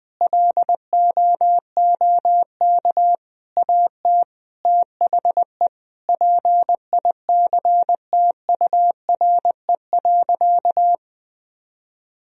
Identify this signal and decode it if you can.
Morse code, heard as a keyed tone, about 20 wpm, 700 Hz. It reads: LOOK AT THE PICTURE.
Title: Sentences from Top 1000 Words - Review Part 1 - 20wpm